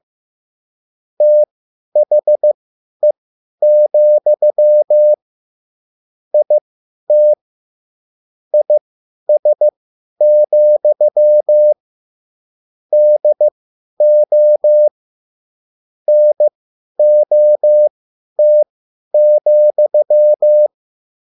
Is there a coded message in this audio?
Yes: Morse code from 1.2 to 20.7 s, heard as a keyed tone, about 15 wpm, spaced as for 10 wpm, 600 Hz. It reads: THE, IT IS, DO NOT,